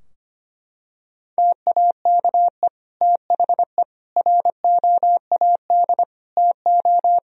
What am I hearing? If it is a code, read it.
Morse code, heard as a keyed tone, about 25 wpm, 700 Hz. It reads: TAKE THE ROAD TO
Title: Sentences from Top 1000 Words - Review Part 2 - 25wpm